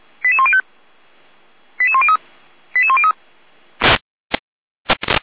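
Five-tone selective call (ZVEI), heard as quick runs of five tones, tones of about 70 ms. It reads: 88105 88103 88103
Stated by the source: ZVEI five-tone selective call received by SDR.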